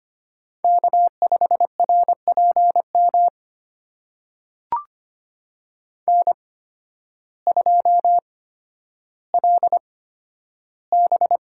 Morse code, heard as a keyed tone, about 25 wpm, spaced as for 12 wpm, 700 Hz. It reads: K5RPM N2LB
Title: USA Call Signs - 5 Characters - 8x Character Spacing - 25wpm